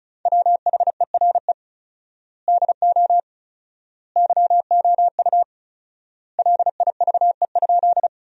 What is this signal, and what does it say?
Morse code, heard as a keyed tone, about 36 wpm, 700 Hz. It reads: WHERE DO YOU LIVE?
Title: 4x Word-Spacing - Sentences from Top 200 Words - 35wpm